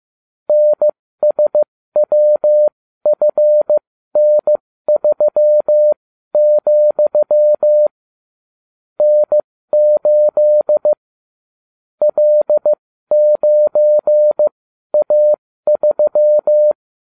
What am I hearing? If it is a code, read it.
Morse code, heard as a keyed tone, about 15 wpm, spaced as for 12 wpm, 600 Hz. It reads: NSWFN3, N8 L9A3